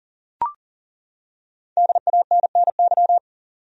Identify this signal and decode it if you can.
Morse code, heard as a keyed tone, about 38 wpm, 700 Hz. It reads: DANNY
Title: USA Common Names - Max 5 Letters - 40wpm